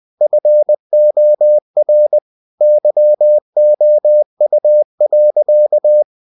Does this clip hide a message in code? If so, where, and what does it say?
Yes: Morse code from 0.2 to 6.0 s, heard as a keyed tone, about 20 wpm, 600 Hz. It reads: FOR YOU.